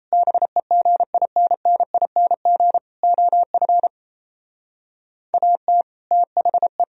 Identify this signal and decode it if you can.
Morse code, heard as a keyed tone, about 34 wpm, 700 Hz. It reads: BEGINNING OF AT THE
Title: Speed-Racing - Sets of 4 Words 28wpm